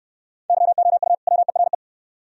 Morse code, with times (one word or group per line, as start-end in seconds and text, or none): 0.5-1.1 YOU
1.3-1.7 WERE